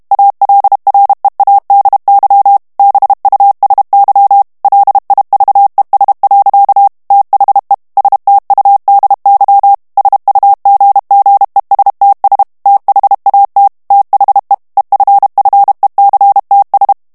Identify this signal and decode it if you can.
Morse code, heard as a keyed tone, about 32 wpm, 800 Hz. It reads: ALREADY BUSY LIVES. THE STUDY SUGGESTS THAT THE EFFECTS
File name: morse mp3 / ANGLAIS texte clair 1 vitesse 30 mots minute